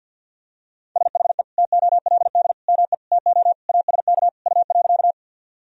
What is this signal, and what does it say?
Morse code, heard as a keyed tone, about 47 wpm, 700 Hz. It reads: SHE TOLD ME TO ASK U.
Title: Sentences from Top 400 Words and Abbreviations 50wpm